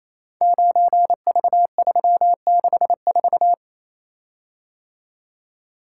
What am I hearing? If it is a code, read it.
Morse code, heard as a keyed tone, about 27 wpm, 700 Hz. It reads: 9V364
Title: Code Groups - 5 Letter:Numbers - 28wpm